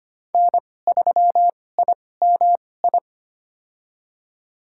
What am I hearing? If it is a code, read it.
Morse code, heard as a keyed tone, about 25 wpm, spaced as for 18 wpm, 700 Hz. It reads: N3IMI
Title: USA Call Signs - 5 Characters - 2x Character Spacing - 25wpm